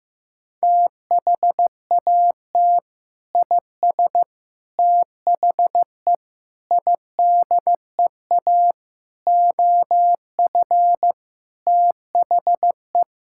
Morse code, heard as a keyed tone, about 15 wpm, 700 Hz. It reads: THAT IS THE IDEA OF THE
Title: Sentences from Top 1000 Words - Review Part 5 - 15wpm